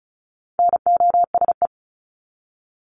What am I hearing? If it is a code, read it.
Morse code, heard as a keyed tone, about 34 wpm, 700 Hz. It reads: NOSE
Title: Top 1501-1750 Words 35wpm